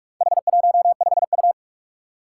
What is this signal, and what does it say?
Morse code, heard as a keyed tone, about 44 wpm, 700 Hz. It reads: H15U
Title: Four Challenging Letter-Number Combinations 45wpm